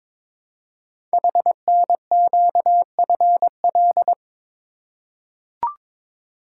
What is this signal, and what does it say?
Morse code, heard as a keyed tone, about 22 wpm, 700 Hz. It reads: HNQFL
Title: Code Groups - 5 Letters - 22wpm